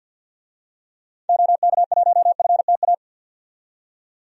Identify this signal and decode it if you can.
Morse code, heard as a keyed tone, about 47 wpm, 700 Hz. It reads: OK1FTA
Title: DXCC-100 Call Signs - 6 Characters - 50wpm